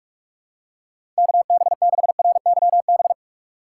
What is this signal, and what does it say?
Morse code, heard as a keyed tone, about 44 wpm, 700 Hz. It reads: KB6RYB
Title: USA Call Signs - 6 Characters - 6x Character Spacing - 45wpm